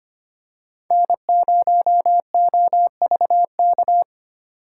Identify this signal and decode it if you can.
Morse code, heard as a keyed tone, about 25 wpm, 700 Hz. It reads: N0OVK
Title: USA Call Signs - 2x Character Spacing - 25wpm